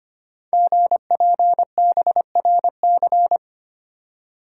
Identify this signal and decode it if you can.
Morse code, heard as a keyed tone, about 25 wpm, 700 Hz. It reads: GPBRC
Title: Code Groups - 5 Letters - 25wpm